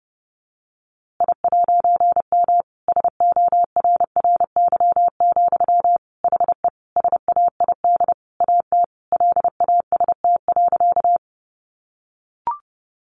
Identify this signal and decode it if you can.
Morse code, heard as a keyed tone, about 30 wpm, 700 Hz. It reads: I'M SORRY, HE SAID AT LAST.
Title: A Sound Of Thunder 30wpm